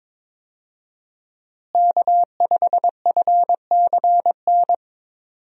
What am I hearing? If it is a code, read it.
Morse code, heard as a keyed tone, about 22 wpm, 700 Hz. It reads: K5FCN
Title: USA Call Signs - 5 Characters - Repeated 3x - 22wpm